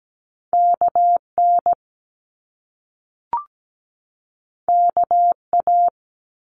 Morse code, heard as a keyed tone, about 17 wpm, 700 Hz. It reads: KN KA
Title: Top Two Letter Combinations in Words 17wpm